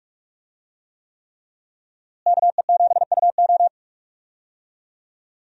Morse code, heard as a keyed tone, about 46 wpm, 700 Hz. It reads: KE7UO
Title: USA Call Signs - 5 Characters - Repeated 3x - 45wpm